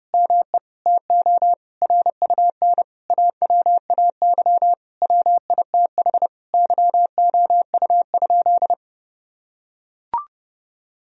Morse code, heard as a keyed tone, about 32 wpm, 700 Hz. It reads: ME TO RUN AWAY WITH YOU?
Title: Sentences from Top 1000 Words - Review Part 3 - 30wpm